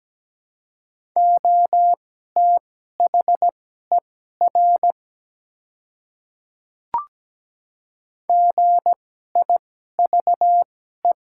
Morse code, heard as a keyed tone, about 17 wpm, spaced as for 12 wpm, 700 Hz. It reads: OTHER GIVE
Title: 2x Character Spacing - Top 200 Words - 17wpm